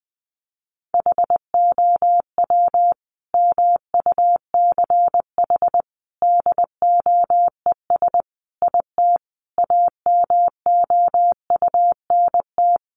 Morse code, heard as a keyed tone, about 20 wpm, 700 Hz. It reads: HOW MUCH DOES IT AMOUNT
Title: Sentences from Top 700 Words 20wpm